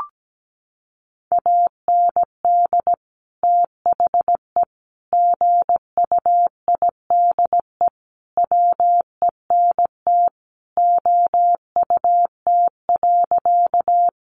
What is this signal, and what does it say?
Morse code, heard as a keyed tone, about 17 wpm, 700 Hz. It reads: AND THE GUIDE WENT OUT.
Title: Sentences from Top 600 Words 17wpm